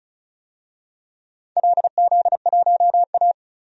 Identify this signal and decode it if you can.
Morse code, heard as a keyed tone, about 36 wpm, 700 Hz. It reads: LZ1A
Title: DXCC-100 Call Signs - 4 Characters - 35wpm